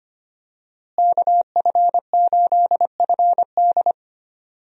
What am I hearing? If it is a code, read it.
Morse code, heard as a keyed tone, about 25 wpm, 700 Hz. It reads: KF8FD
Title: USA Call Signs - 5 Characters - Repeated 3x - 25wpm